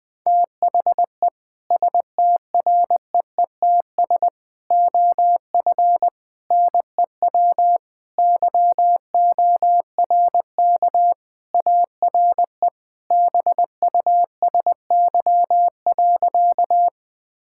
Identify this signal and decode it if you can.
Morse code, heard as a keyed tone, about 20 wpm, 700 Hz. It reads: THE STREETS OF NEW YORK ARE BUSY.
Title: Sentences with 3-Word Phrases - 20wpm